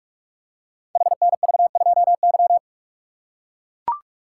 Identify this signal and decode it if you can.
Morse code, heard as a keyed tone, about 44 wpm, 690 Hz. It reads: HNV2Y
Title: Code Groups - 5 Letter:Numbers - 45wpm